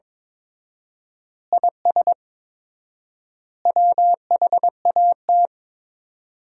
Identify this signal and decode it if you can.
Morse code, heard as a keyed tone, about 22 wpm, 700 Hz. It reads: IS WHAT
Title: 4x Word-Spacing - Sentences from Top 100 Words - 22wpm